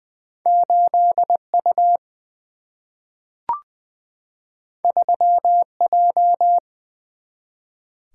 Morse code, heard as a keyed tone, about 20 wpm, 700 Hz. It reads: 8U 3J
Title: Two Letter-Number Combinations - 20wpm